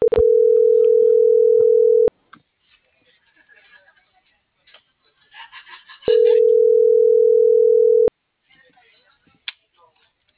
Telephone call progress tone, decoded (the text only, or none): ringback tone